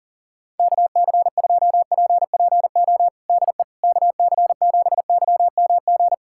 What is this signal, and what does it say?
Morse code, heard as a keyed tone, about 40 wpm, 700 Hz. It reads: KC2PPO DE KC7YMG
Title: USA Callsign DE Callsign - 6 Characters - 40wpm